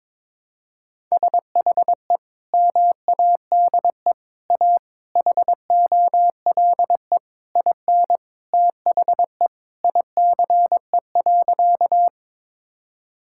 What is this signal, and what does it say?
Morse code, heard as a keyed tone, about 22 wpm, 700 Hz. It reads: SHE MADE A HOLE IN THE ICE.
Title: Sentences from Top 1600 Words 22wpm